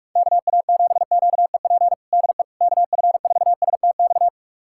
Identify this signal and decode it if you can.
Morse code, heard as a keyed tone, about 41 wpm, 700 Hz. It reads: KA7QEP DE KF4STX
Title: USA Callsign DE Callsign 45wpm